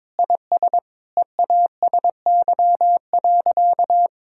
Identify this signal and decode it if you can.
Morse code, heard as a keyed tone, about 22 wpm, 700 Hz. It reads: IS EASY.
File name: Sentences with 7-Word Phrases - 22wpm